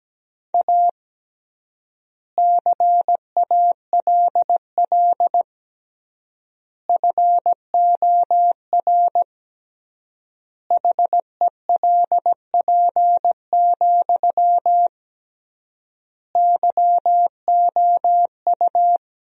Morse code, heard as a keyed tone, about 17 wpm, 700 Hz. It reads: A CALL FOR HELP, YOU
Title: 3x Word-Spacing - Sentences from Top 300 Words - 17wpm